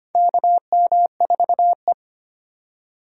Morse code, heard as a keyed tone, about 25 wpm, 700 Hz. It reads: KM4E
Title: USA Call Signs - 4 Characters - 3x Character Spacing - 25wpm